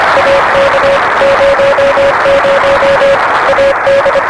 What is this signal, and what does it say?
Morse code, heard as a keyed tone, about 25 wpm, 560 Hz, with noise. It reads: AK 00 AD
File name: RS-7 Beacon 16 Feb 1985 oz6bl